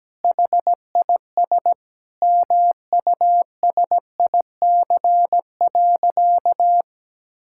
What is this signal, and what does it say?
Morse code, heard as a keyed tone, about 17 wpm, 700 Hz. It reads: HIS MUSIC.
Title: Sentences from Top 1000 Words - Review Part 6 - 17wpm